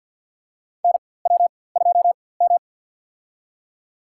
Morse code, heard as a keyed tone, about 47 wpm, spaced as for 23 wpm, 700 Hz. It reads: NW2M
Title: USA Call Signs - 4x Character Spacing - 50wpm